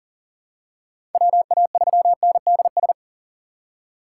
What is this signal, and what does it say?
Morse code, heard as a keyed tone, about 40 wpm, 690 Hz. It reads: WA3NDS
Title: USA Call Signs - 6x Character Spacing - 40wpm